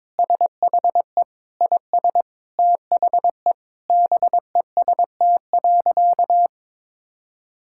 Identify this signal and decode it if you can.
Morse code, heard as a keyed tone, about 23 wpm, 700 Hz. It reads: SHE IS THE BEST.
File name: practice / 22wpm / Sentences with 3-Word Phrases - 22wpm